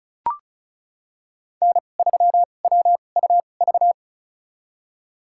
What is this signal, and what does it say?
Morse code, heard as a keyed tone, about 34 wpm, spaced as for 25 wpm, 700 Hz. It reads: N3WUV
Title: USA Call Signs - 5 Characters - 2x Character Spacing - 35wpm